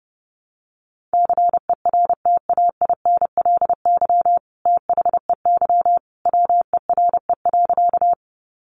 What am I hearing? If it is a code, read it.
Morse code, heard as a keyed tone, about 29 wpm, 700 Hz. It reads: CERTAINLY THEY WERE.
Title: Sentences from Top 900 Words 30wpm